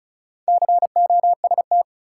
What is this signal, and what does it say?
Morse code, heard as a keyed tone, about 36 wpm, 700 Hz. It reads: COST
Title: Sets of 3 Words - New Word Drill - 35wpm